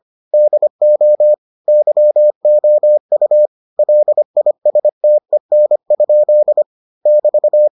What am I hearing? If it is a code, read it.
Morse code, heard as a keyed tone, about 25 wpm, 600 Hz. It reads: DO YOU LISTEN? <BT>